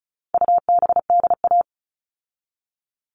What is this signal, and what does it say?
Morse code, heard as a keyed tone, about 34 wpm, 700 Hz. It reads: UBDA
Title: All Cut Numbers - 4-Digit 35wpm